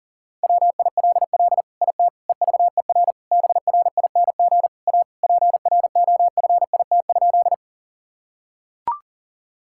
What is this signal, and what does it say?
Morse code, heard as a keyed tone, about 40 wpm, 700 Hz. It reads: WILL IT EVER BRING A PROFIT?
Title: Sentences from Top 1000 Words - Review Part 5 - 40wpm